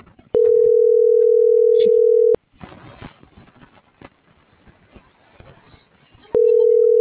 Ringback tone (North America), 440 + 480 Hz.